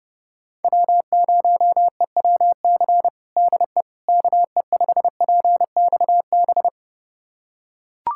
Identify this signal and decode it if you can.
Morse code, heard as a keyed tone, about 30 wpm, 690 Hz. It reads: W0EWC DE KE5PXB
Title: USA Callsign DE Callsign 30wpm